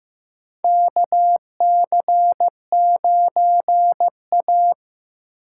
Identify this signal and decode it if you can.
Morse code, heard as a keyed tone, about 15 wpm, 700 Hz. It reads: KC9A
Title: USA Call Signs - 4 Characters - 6x Character Spacing - 15wpm